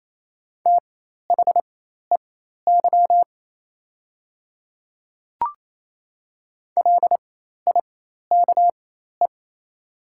Morse code, heard as a keyed tone, about 27 wpm, spaced as for 13 wpm, 700 Hz. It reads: THEY LIKE
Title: 4x Character Spacing - Top 200 Words - 28wpm